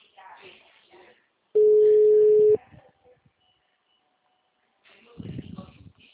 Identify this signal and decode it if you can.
Ringback tone (Europe, 425 Hz), with speech.
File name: out- -2003- - - .27